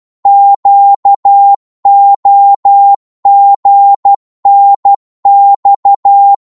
Morse code, heard as a keyed tone, about 12 wpm, 800 Hz. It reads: QOGNX
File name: Morse / ltr-12wpm